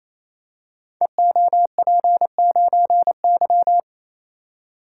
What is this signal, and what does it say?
Morse code, heard as a keyed tone, about 29 wpm, 700 Hz. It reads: EOP9Y
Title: Code Groups - 5 Letter:Numbers - 28wpm